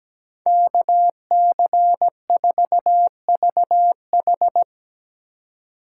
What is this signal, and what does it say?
Morse code, heard as a keyed tone, about 19 wpm, 700 Hz. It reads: KC4VH
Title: Speed Racing - USA 2x2 Call Signs 17wpm